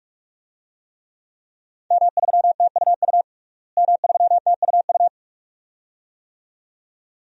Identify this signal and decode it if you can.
Morse code, heard as a keyed tone, about 46 wpm, 700 Hz. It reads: M3TUU M3TUU
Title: DXCC-100 Call Signs - 5 Characters - Repeated 2x - 45wpm